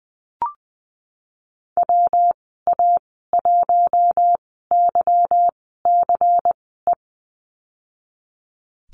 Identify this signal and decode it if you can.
Morse code, heard as a keyed tone, about 20 wpm, spaced as for 15 wpm, 700 Hz. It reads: WA1YCE
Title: 186 Lesson 30g - Letter b - Call Signs 20wpm